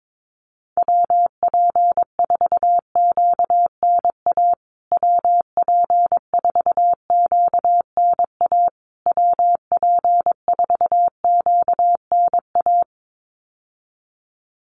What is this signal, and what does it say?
Morse code, heard as a keyed tone, about 22 wpm, 700 Hz. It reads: WP4QNA WP4QNA WP4QNA
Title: USA Call Signs Repeated 3x 22wpm